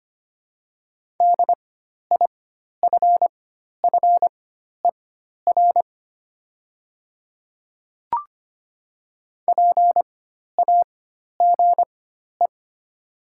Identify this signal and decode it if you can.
Morse code, heard as a keyed tone, about 25 wpm, spaced as for 12 wpm, 700 Hz. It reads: DIFFER PAGE
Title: 4x Character Spacing - Top 300 Words - 25wpm